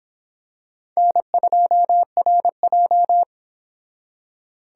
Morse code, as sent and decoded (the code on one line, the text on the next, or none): -. ..--- .-. .---
N2RJ